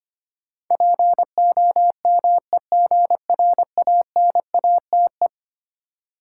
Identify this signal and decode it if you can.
Morse code, heard as a keyed tone, about 25 wpm, 700 Hz. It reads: POMEGRANATE
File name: practice / 25wpm / Difficult Words with Hints - 25wpm